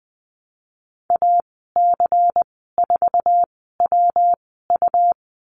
Morse code, heard as a keyed tone, about 20 wpm, spaced as for 15 wpm, 700 Hz. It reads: AC4WU
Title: 147 Lesson 24g - Letter y - Call Signs 20wpm